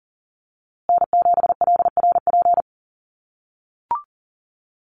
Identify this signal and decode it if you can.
Morse code, heard as a keyed tone, about 40 wpm, 700 Hz. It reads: N7LRP
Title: USA Call Signs Repeated 2x 40wpm